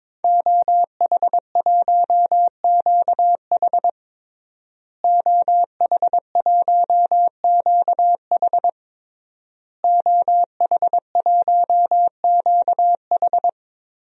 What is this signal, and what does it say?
Morse code, heard as a keyed tone, about 22 wpm, 700 Hz. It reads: OH1QH OH1QH OH1QH
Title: DXCC-100 Call Signs - 5 Characters - Repeated 3x - 22wpm